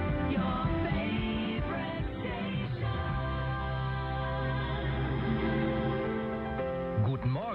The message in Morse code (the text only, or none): none